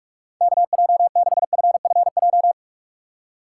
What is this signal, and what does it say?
Morse code, heard as a keyed tone, about 44 wpm, 700 Hz. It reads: KJ6FFJ